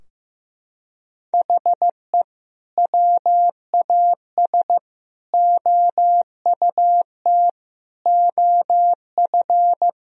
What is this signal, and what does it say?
Morse code, heard as a keyed tone, about 15 wpm, 700 Hz. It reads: HE WAS OUT OF
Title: Sentences from Top 1000 Words - Review Part 4 - 15wpm